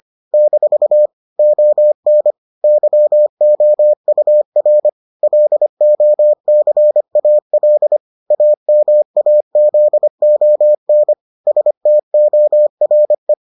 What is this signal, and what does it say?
Morse code, heard as a keyed tone, about 25 wpm, 600 Hz. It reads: - ON YOUR LOCAL AMAZON STORE